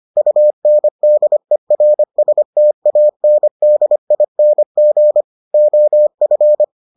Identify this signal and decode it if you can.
Morse code, heard as a keyed tone, about 25 wpm, 600 Hz. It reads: UNDERSTANDING OF